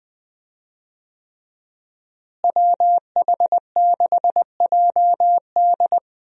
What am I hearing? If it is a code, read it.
Morse code, heard as a keyed tone, about 20 wpm, 700 Hz. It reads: WH6JD
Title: USA Call Signs - 5 Characters - Repeated 3x - 20wpm